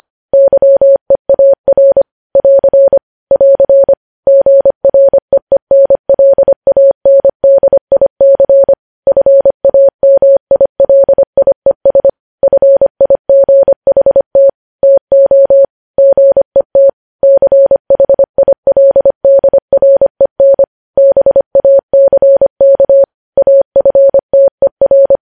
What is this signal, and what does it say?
Morse code, heard as a keyed tone, about 25 wpm, 550 Hz. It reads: YEAR <AR> <AR> GREENLANDIC FAMILIES FIGHT TO GET CHILDREN BACK AFTER
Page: News Headlines in Morse Code at 25 WPM